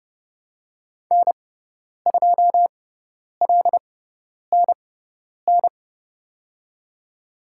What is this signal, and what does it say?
Morse code, heard as a keyed tone, about 31 wpm, spaced as for 10 wpm, 700 Hz. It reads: N2LNN
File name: Lesson 17g - Letter u - Call Signs 30wpm